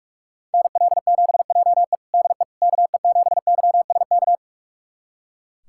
Morse code, heard as a keyed tone, about 45 wpm, 700 Hz. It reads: NL7JE DE KE7YSK
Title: USA Callsign DE Callsign 45wpm